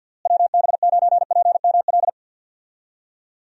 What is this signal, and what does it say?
Morse code, heard as a keyed tone, about 51 wpm, 700 Hz. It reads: WB9PML
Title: USA Call Signs - 6 Characters - Repeated 2x - 50wpm